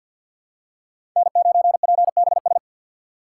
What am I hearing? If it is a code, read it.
Morse code, heard as a keyed tone, about 51 wpm, 700 Hz. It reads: N9PBS
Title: USA Call Signs - 5 Characters - 2x Character Spacing - 50wpm